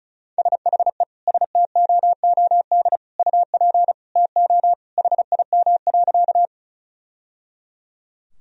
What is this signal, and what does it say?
Morse code, heard as a keyed tone, about 36 wpm, 700 Hz. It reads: SHE STOOD UP TO HIM.
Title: Sentences from Top 1000 Words - Review Part 2 - 35wpm